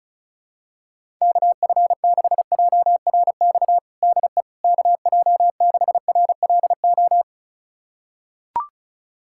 Morse code, heard as a keyed tone, about 36 wpm, 700 Hz. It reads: KF6JRX DE KJ6RLO
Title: USA Callsign DE Callsign 35wpm